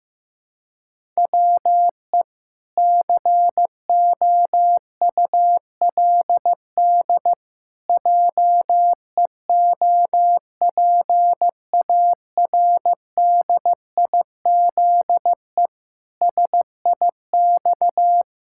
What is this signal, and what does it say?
Morse code, heard as a keyed tone, about 16 wpm, 700 Hz. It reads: WE COULD JEOPARDIZE SIX
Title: Pangram Challenge v2 - 15wpm